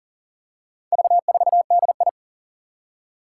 Morse code, heard as a keyed tone, about 42 wpm, 700 Hz. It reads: V4DI
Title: Four Challenging Letter-Number Combinations 40wpm